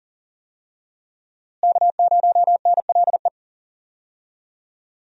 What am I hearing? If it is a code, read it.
Morse code, heard as a keyed tone, about 37 wpm, 700 Hz. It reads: K0NLE
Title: USA Call Signs - 5 Characters - Repeated 3x - 40wpm